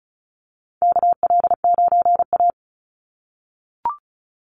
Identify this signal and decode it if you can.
Morse code, heard as a keyed tone, about 34 wpm, 700 Hz. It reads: KL9A
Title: Call Signs - 35wpm